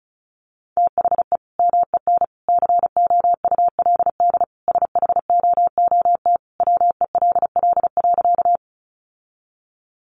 Morse code, heard as a keyed tone, about 34 wpm, 700 Hz. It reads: THE MEN COULD SHOOT WELL.
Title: Sentences from Top 1200 Words 35wpm